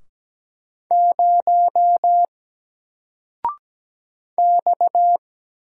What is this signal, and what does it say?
Morse code, heard as a keyed tone, about 17 wpm, 700 Hz. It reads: 0 X
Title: Single Letter-Number Alt 17wpm